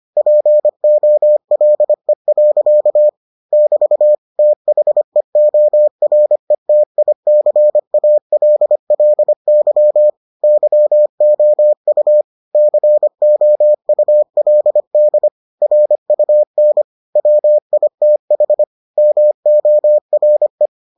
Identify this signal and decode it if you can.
Morse code, heard as a keyed tone, about 25 wpm, 600 Hz. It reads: POLE. <BT> THEORETICALLY YOU COULD RUN WITH MORE